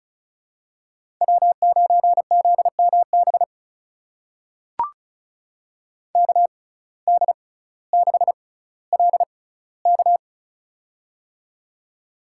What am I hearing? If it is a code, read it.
Morse code, heard as a keyed tone, about 36 wpm, spaced as for 18 wpm, 700 Hz. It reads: W9ZMB KD6LK
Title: USA Call Signs - 5 Characters - 6x Character Spacing - 35wpm